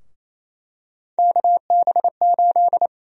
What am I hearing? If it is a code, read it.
Morse code, heard as a keyed tone, about 28 wpm, 700 Hz. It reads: KB8
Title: USA Call Sign Prefixes - Repeated 2x - 28wpm